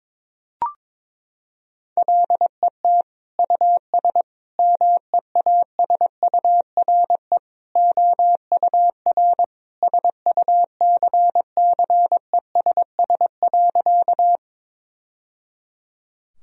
Morse code, heard as a keyed tone, about 22 wpm, 700 Hz. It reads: LET US MEASURE OUR SUCCESS.
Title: Sentences from Top 1000 Words - Review Part 2 - 22wpm